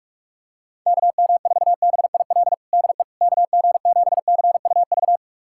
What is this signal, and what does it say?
Morse code, heard as a keyed tone, about 46 wpm, 700 Hz. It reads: KM4BIL DE KG7CUV
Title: USA Callsign DE Callsign - 6 Characters - 45wpm